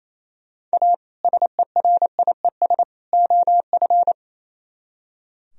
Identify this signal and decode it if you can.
Morse code, heard as a keyed tone, about 34 wpm, 700 Hz. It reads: A SERIES OF
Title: Speed-Racing - Sets of 3 Words 28wpm